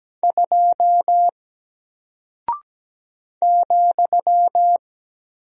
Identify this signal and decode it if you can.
Morse code, heard as a keyed tone, about 17 wpm, 700 Hz. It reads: 2 ,
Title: Single Character (Letter:Number:Punctuation:Prosign) Alt 17wpm